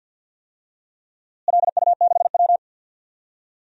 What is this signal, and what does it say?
Morse code, heard as a keyed tone, about 47 wpm, 700 Hz. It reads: LU6W
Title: DXCC-100 Call Signs - 4 Characters - Repeated 3x - 50wpm